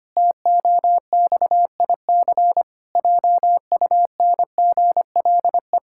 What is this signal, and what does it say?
Morse code, heard as a keyed tone, about 25 wpm, 700 Hz. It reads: TOXIC JUNGLE
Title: Pangram Challenge v2 - 25wpm